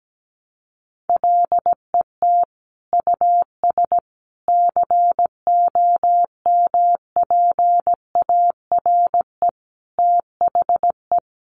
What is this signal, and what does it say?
Morse code, heard as a keyed tone, about 17 wpm, 700 Hz. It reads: LET US COMPARE THE
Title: Sentences from Top 1300 Words 17wpm